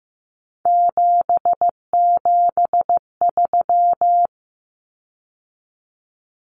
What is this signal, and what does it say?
Morse code, heard as a keyed tone, about 15 wpm, 700 Hz. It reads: 773
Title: Common Cut Numbers - 3-Digit 15wpm